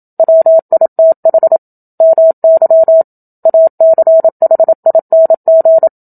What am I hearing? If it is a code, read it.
Morse code, heard as a keyed tone, about 27 wpm, 650 Hz. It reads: WITH MY ACHING
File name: Book Two / 01-Chapter 1 - 27 WPM